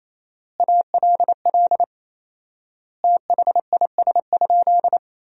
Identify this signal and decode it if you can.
Morse code, heard as a keyed tone, about 28 wpm, 700 Hz. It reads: ALL THIS?
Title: 4x Word-Spacing - Sentences from Top 100 Words - 28wpm